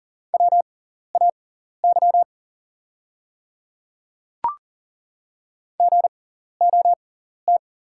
Morse code, heard as a keyed tone, about 40 wpm, spaced as for 14 wpm, 690 Hz. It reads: WAY GOT
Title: 6x Character Spacing - Top 300 Words - 40wpm